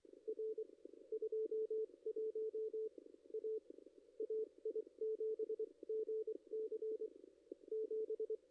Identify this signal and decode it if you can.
Morse code, heard as a keyed tone, about 24 wpm, spaced as for 21 wpm, 440 Hz. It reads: R 21 A AI7GC 7
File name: contests / ss-cw-k5tr-op-n5rz -2300